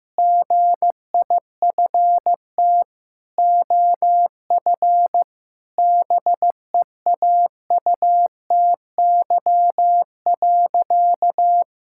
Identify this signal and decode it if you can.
Morse code, heard as a keyed tone, about 15 wpm, 700 Hz. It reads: GIFT OF BEAUTY.
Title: Sentences from Top 1000 Words - Review Part 6 - 15wpm